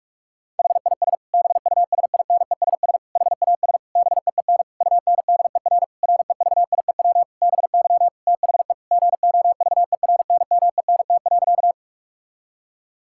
Morse code, heard as a keyed tone, about 41 wpm, 700 Hz. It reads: HIS BUSINESS HAS BEEN UNDER REVIEW BY THE GOVERNMENT.
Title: Sentences from Top 1000 Words - Review Part 6 - 45wpm